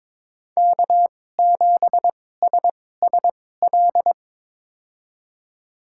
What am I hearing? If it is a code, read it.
Morse code, heard as a keyed tone, about 22 wpm, spaced as for 16 wpm, 700 Hz. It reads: K7SSL
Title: USA Call Signs - 5 Characters - 2x Character Spacing - 22wpm